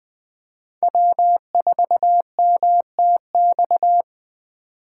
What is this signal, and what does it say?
Morse code, heard as a keyed tone, about 20 wpm, 700 Hz. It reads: W4MTX